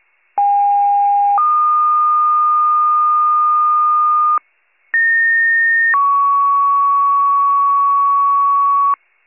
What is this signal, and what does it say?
Two-tone page, heard as a one-second tone, then a three-second tone, about 810 then 1220 Hz, about 1850 then 1120 Hz. It is heard from 0.4 to 8.9 s.